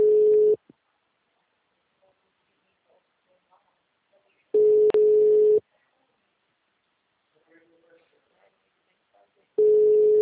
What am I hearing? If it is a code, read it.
Ringback tone (Europe, 425 Hz), with speech.